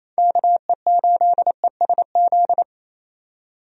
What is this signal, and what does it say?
Morse code, heard as a keyed tone, about 27 wpm, 700 Hz. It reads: KE8ESZ
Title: USA Call Signs - 6 Characters - 28wpm